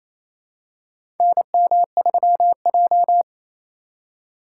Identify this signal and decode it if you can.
Morse code, heard as a keyed tone, about 27 wpm, 700 Hz. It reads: NM3J
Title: USA Call Signs - 4 Characters - 4x Character Spacing - 28wpm